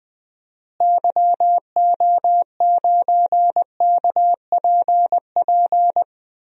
Morse code, heard as a keyed tone, about 21 wpm, 700 Hz. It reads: YO9KPP
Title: DXCC-100 Call Signs - 6 Characters - Repeated 2x - 20wpm